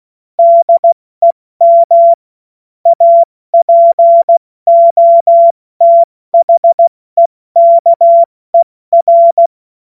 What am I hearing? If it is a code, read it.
Morse code, heard as a keyed tone, about 16 wpm, 680 Hz. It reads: DEM APOTHEKER
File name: Morsekurs / Witze / Witz 05 März 2024 70 F80